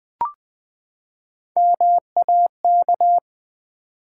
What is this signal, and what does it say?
Morse code, heard as a keyed tone, about 20 wpm, 700 Hz. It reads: MAK
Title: Build A Word - 3 Letters At A Time - 6-Letter Words - 20wpm